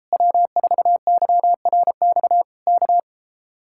Morse code, heard as a keyed tone, about 33 wpm, 700 Hz. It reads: W4YRX K